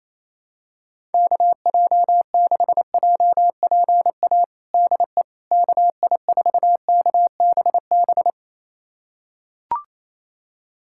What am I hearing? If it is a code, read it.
Morse code, heard as a keyed tone, about 28 wpm, 700 Hz. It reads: KJ6JPA DE KI4KBB
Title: USA Callsign DE Callsign - 6 Characters - 28wpm